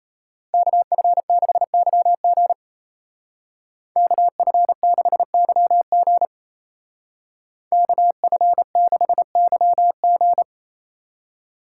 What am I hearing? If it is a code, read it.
Morse code, heard as a keyed tone, about 32 wpm, 700 Hz. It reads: KF6YG KF6YG KF6YG
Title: Speed Racing - USA 2x2 Call Signs 28wpm